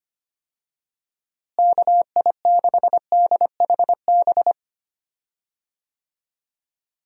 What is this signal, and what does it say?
Morse code, heard as a keyed tone, about 25 wpm, 700 Hz. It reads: KI6DHB
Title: USA Call Signs - 6 Characters - Repeated 3x - 25wpm